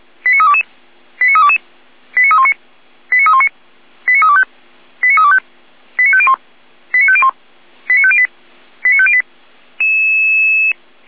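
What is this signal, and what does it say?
Five-tone selective call (ZVEI), heard as quick runs of five tones, tones of about 70 ms. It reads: 89322 89322 89319 89319 89325 89325 89501 89501 89508 89508
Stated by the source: ZVEI five-tone selective call received by SDR.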